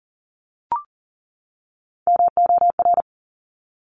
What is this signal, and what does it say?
Morse code, heard as a keyed tone, about 41 wpm, 690 Hz. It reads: MOR
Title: Top Three Letter Combinations in Words 40wpm